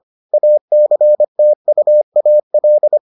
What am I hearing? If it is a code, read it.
Morse code, heard as a keyed tone, about 25 wpm, 600 Hz. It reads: ACTUAL